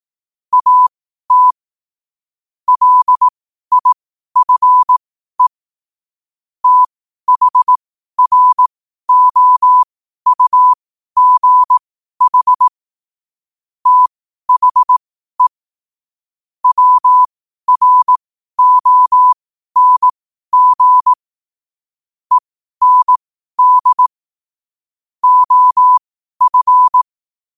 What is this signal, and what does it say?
Morse code, heard as a keyed tone, about 18 wpm, spaced as for 12 wpm, 1000 Hz. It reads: AT LIFE THROUGH THE WRONG END OF
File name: qotdfiles / 12wpm